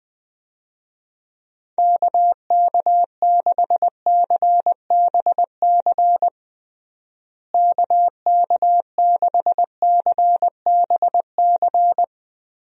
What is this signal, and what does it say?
Morse code, heard as a keyed tone, about 20 wpm, 700 Hz. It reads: KK6CBC KK6CBC
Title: USA Call Signs - 6 Characters - Repeated 3x - 20wpm